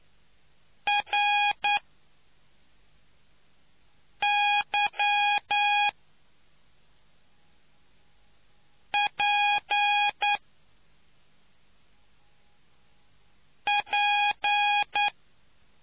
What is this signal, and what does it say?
Morse code, heard as a keyed tone, about 9 wpm, spaced as for 3 wpm, 820 Hz. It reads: RYPP